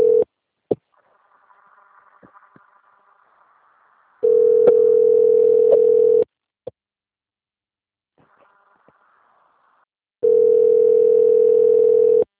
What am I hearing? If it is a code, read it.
Ringback tone (North America), 440 + 480 Hz.